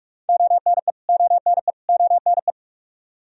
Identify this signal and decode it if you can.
Morse code, heard as a keyed tone, about 46 wpm, 700 Hz. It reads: ONE ONE ONE
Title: Top 100 Words Repeated 45wpm